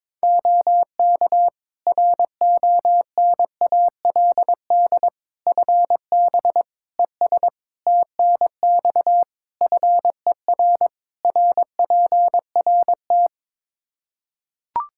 Morse code, heard as a keyed tone, about 22 wpm, 700 Hz. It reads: OK RONALD FB ES TNX FER RPRT
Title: Practice QSO - Exchange 2 - Info - 22wpm